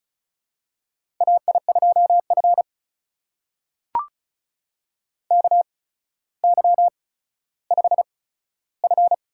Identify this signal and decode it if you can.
Morse code, heard as a keyed tone, about 36 wpm, spaced as for 15 wpm, 700 Hz. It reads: AI2F KY5F
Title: USA Call Signs - 4 Characters - 8x Character Spacing - 35wpm